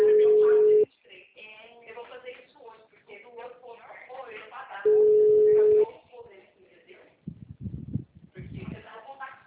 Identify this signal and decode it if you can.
Ringback tone (Europe, 425 Hz), with speech.
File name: out- -2020- - - .6776